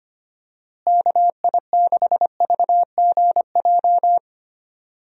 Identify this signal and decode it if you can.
Morse code, heard as a keyed tone, about 25 wpm, 700 Hz. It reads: KI6VGJ
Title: USA Call Signs - 6 Characters - Repeated 2x - 25wpm